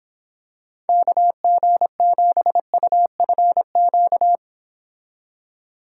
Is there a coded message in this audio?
Yes: Morse code from 0.9 to 4.4 s, heard as a keyed tone, about 26 wpm, 700 Hz. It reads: KG7UFQ